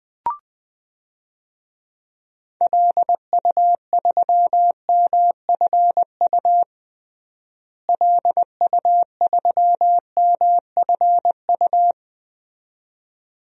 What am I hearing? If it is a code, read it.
Morse code, heard as a keyed tone, about 20 wpm, 700 Hz. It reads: LU3MFU LU3MFU
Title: DXCC-100 Call Signs - 6 Characters - Repeated 2x - 20wpm